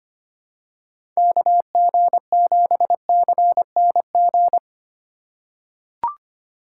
Morse code, heard as a keyed tone, about 25 wpm, 700 Hz. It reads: KG7CNG
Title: USA Call Signs - 2x Character Spacing - 25wpm